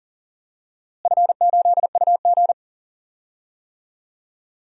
Morse code, heard as a keyed tone, about 40 wpm, 700 Hz. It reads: F8UG
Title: DXCC-100 Call Signs - 4 Characters - Repeated 2x - 40wpm